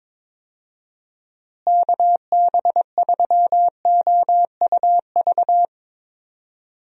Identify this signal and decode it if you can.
Morse code, heard as a keyed tone, about 21 wpm, 700 Hz. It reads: KB3OUV
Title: USA Call Signs - 6 Characters - Repeated 2x - 22wpm